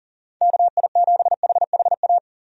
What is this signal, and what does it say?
Morse code, heard as a keyed tone, about 40 wpm, 700 Hz. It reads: KI7HHA
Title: USA Call Signs - 6 Characters - 40wpm